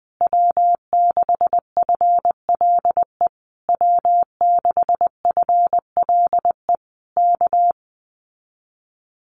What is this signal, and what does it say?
Morse code, heard as a keyed tone, about 20 wpm, 700 Hz. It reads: W6FLE W6FLE K
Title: Common QSO Phrases Encore 20wpm